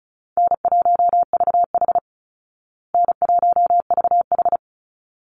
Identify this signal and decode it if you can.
Morse code, heard as a keyed tone, about 34 wpm, 700 Hz. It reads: N1VH N1VH
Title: Easy Call Signs Repeated 2x - 35wpm